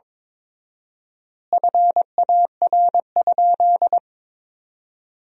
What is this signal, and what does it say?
Morse code, heard as a keyed tone, about 22 wpm, 700 Hz. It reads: FAR?
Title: 4x Word-Spacing - Sentences from Top 300 Words - 22wpm